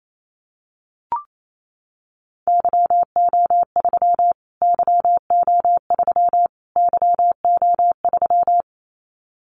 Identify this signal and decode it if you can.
Morse code, heard as a keyed tone, about 28 wpm, 700 Hz. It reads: YO3 YO3 YO3
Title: Callsign Prefixes for Top 25 DXCC - Repeated 3x - 28wpm